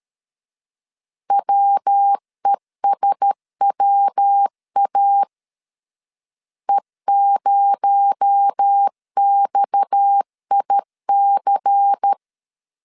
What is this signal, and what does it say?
Morse code, heard as a keyed tone, about 13 wpm, 800 Hz. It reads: WESWA E0XIC
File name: CW / DET / 1 6 1500C S B 12wpm 3431